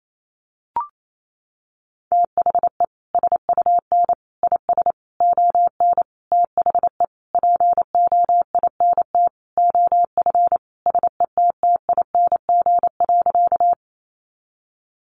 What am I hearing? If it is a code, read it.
Morse code, heard as a keyed tone, about 28 wpm, 700 Hz. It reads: THE SUN IS ON THE POINT OF SETTING.
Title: Sentences from Top 1300 Words 28wpm